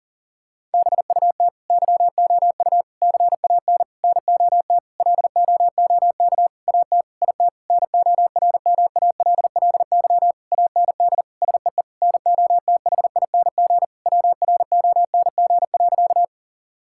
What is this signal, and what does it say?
Morse code, heard as a keyed tone, about 40 wpm, 700 Hz. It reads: BUT YOU CAN NOT LOOK AT IT NORMALLY AND SEE NOTHING WRONG.
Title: Sentences from Top 1000 Words - Review Part 6 - 40wpm